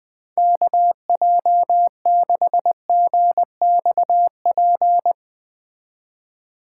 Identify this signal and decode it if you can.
Morse code, heard as a keyed tone, about 20 wpm, 700 Hz. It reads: KJ6GXP
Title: Speed Racing - USA 2x3 Call Signs 17wpm